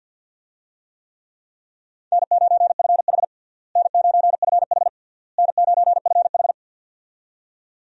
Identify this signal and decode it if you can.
Morse code, heard as a keyed tone, about 52 wpm, 700 Hz. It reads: N9FH N9FH N9FH
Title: USA Call Signs - 4 Characters - Repeated 3x - 50wpm